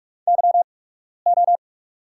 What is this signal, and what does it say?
Morse code, heard as a keyed tone, about 44 wpm, spaced as for 12 wpm, 690 Hz. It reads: YO
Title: 8x Character Spacing - Top 500 Words - 45wpm